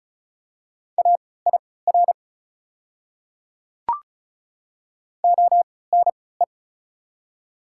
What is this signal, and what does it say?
Morse code, heard as a keyed tone, about 35 wpm, spaced as for 20 wpm, 700 Hz. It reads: AIR ONE
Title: 3x Character Spacing - Top 200 Words - 35wpm